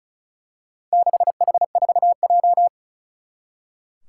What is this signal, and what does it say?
Morse code, heard as a keyed tone, about 34 wpm, 700 Hz. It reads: 6H4J
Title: Four Challenging Letter-Number Combinations 35wpm